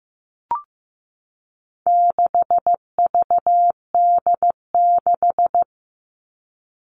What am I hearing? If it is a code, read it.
Morse code, heard as a keyed tone, about 15 wpm, 700 Hz. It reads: 6VD6
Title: All Cut Numbers - 4-Digit 15wpm